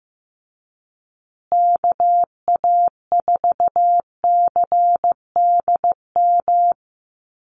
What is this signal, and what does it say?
Morse code, heard as a keyed tone, about 15 wpm, 700 Hz. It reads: KA4CDM
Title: USA Call Signs Repeated 2x 15wpm